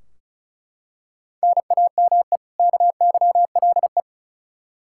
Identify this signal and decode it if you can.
Morse code, heard as a keyed tone, about 34 wpm, 700 Hz. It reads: NAME KYLE
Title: Practice QSO - Exchange 1 - Beginning - 35wpm